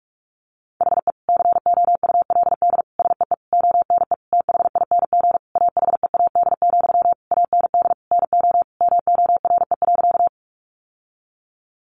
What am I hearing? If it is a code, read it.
Morse code, heard as a keyed tone, about 44 wpm, 700 Hz. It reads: HE COULD SEE ONE THING AHEAD, AND NO MORE.
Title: Sentences from Top 1100 Words 45wpm